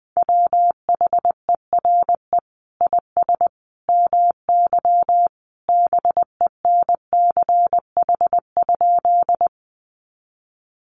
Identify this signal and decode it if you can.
Morse code, heard as a keyed tone, about 20 wpm, 700 Hz. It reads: WHERE IS MY BENCH?
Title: Sentences from Top 1400 Words 20wpm